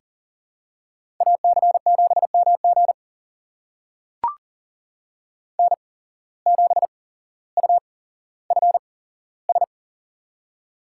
Morse code, heard as a keyed tone, about 39 wpm, spaced as for 17 wpm, 700 Hz. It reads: AC7MG N7UFS
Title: USA Call Signs - 5 Characters - 8x Character Spacing - 40wpm